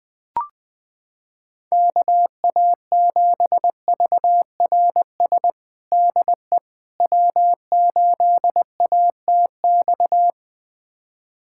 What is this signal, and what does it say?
Morse code, heard as a keyed tone, about 20 wpm, 700 Hz. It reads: KA7VRS DE W8ATX
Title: USA Callsign DE Callsign 20wpm